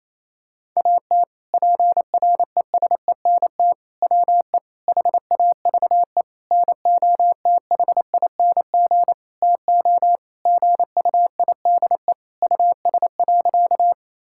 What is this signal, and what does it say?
Morse code, heard as a keyed tone, about 28 wpm, 700 Hz. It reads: AT PRESENT WE HAVE NOTHING TO GUIDE US.
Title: Sentences from Top 1000 Words - Review Part 5 - 28wpm